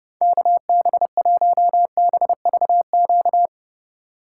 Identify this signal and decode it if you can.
Morse code, heard as a keyed tone, about 36 wpm, 700 Hz. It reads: KB1BVQ
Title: Speed Racing - USA 2x3 Call Signs 30wpm